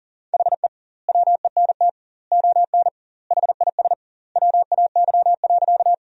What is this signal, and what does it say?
Morse code, heard as a keyed tone, about 40 wpm, 700 Hz. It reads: HE WENT ON HIS WAY.
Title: 2x Word-Spacing - Sentences from Top 200 Words - 40wpm